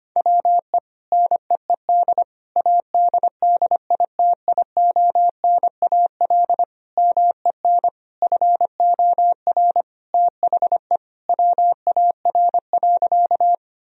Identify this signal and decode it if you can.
Morse code, heard as a keyed tone, about 25 wpm, 700 Hz. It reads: WE NEED ADDITIONAL MEN FOR THE WAR.
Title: Sentences from Top 1000 Words - Review Part 4 - 25wpm